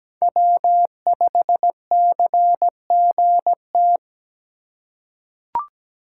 Morse code, heard as a keyed tone, about 17 wpm, 700 Hz. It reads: W5CGT
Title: USA Call Signs - 5 Characters - 3x Character Spacing - 17wpm